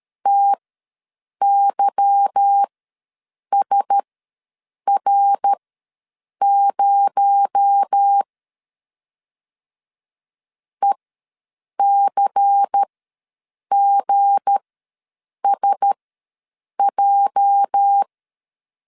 Morse code, heard as a keyed tone, about 13 wpm, spaced as for 7 wpm, 800 Hz. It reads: TYSR0 ECGSJ
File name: LEZIONI / 8 YJ45 / 8 YJ45 ws 12wpm 3146